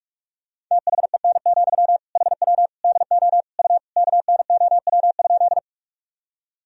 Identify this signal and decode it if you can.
Morse code, heard as a keyed tone, about 44 wpm, 700 Hz. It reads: THEN, HW DO U KNOW?
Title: Sentences from Top 100 Words and Abbreviations 45wpm